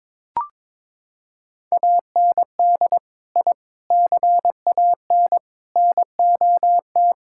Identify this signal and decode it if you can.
Morse code, heard as a keyed tone, about 22 wpm, 700 Hz. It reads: AND I CAN NOT
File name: practice / 22wpm / Sentences with 4-Word Phrases - 22wpm